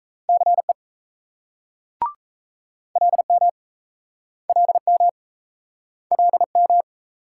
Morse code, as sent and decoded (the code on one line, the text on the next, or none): -.-. / .-.. -- / .-.. -- / .-.. --
CE LM LM LM